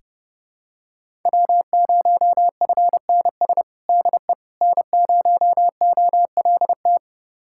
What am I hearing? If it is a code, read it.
Morse code, heard as a keyed tone, about 30 wpm, 700 Hz. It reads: W0FNS DE N0OLT
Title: USA Callsign DE Callsign - 5 Characters - 30wpm